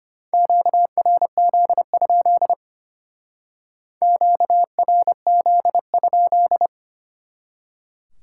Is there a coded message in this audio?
Yes: Morse code from 0.3 to 6.7 s, heard as a keyed tone, about 33 wpm, 700 Hz. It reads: QRZ? QRZ?